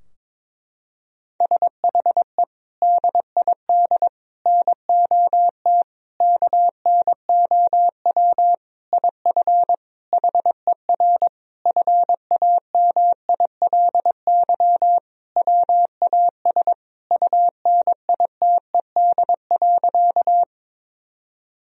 Morse code, heard as a keyed tone, about 22 wpm, 700 Hz. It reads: SHE DID NOT KNOW IF HER FAMILY WAS UNITED.
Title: Sentences from Top 1000 Words - Review Part 4 - 22wpm